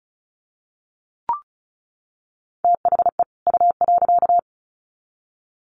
Morse code, heard as a keyed tone, about 34 wpm, 700 Hz. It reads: THE U.
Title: Sets of 2 Words 35wpm